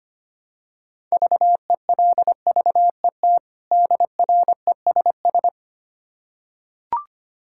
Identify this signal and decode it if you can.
Morse code, heard as a keyed tone, about 25 wpm, 700 Hz. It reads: VELVET DRESS
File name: practice / 25wpm / Difficult Words with Hints - 25wpm